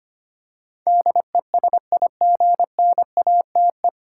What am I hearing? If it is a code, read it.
Morse code, heard as a keyed tone, about 25 wpm, 700 Hz. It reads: DESIGNATE
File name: Nine Letter Words - 25wpm